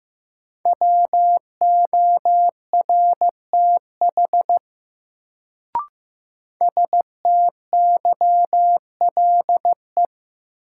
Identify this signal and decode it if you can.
Morse code, heard as a keyed tone, about 15 wpm, 700 Hz. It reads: WORTH STYLE
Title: Sentences from Top 800 Words - New Word Drill - 15wpm